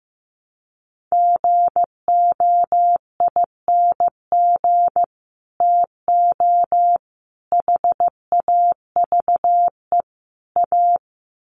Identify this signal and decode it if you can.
Morse code, heard as a keyed tone, about 15 wpm, 700 Hz. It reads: GOING TO HAVE A
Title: Sets of 4 Words 15wpm